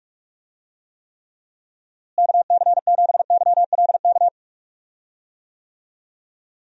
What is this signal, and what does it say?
Morse code, heard as a keyed tone, about 41 wpm, 700 Hz. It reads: KC7YLK
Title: USA Call Signs - 6 Characters - Repeated 2x - 45wpm